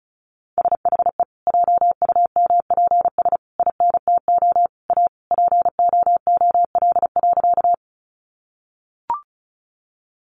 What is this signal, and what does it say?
Morse code, heard as a keyed tone, about 34 wpm, 700 Hz. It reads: SHE JUMPS INTO A POOL.
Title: Easy Phrases 35wpm